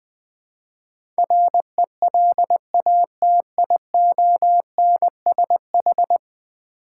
Morse code, heard as a keyed tone, about 21 wpm, 700 Hz. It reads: RELATIONSH
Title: Build A Word - 5 Letters At A Time - 12-Letter Words - 20wpm